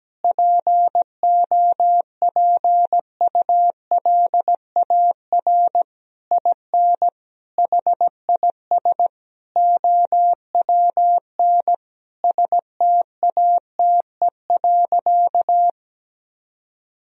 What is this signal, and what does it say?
Morse code, heard as a keyed tone, about 17 wpm, 700 Hz. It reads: POPULAR IN HIS OWN STATE.
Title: Sentences from Top 1000 Words - Review Part 3 - 17wpm